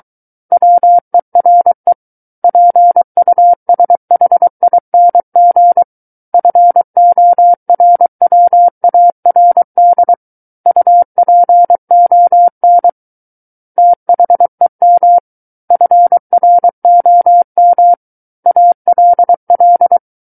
Morse code, heard as a keyed tone, about 23 wpm, 700 Hz. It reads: WERE PUSHING FORWARD UPON THEM FROM ALL
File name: Book One / 14-Chapter 14 - 23 WPM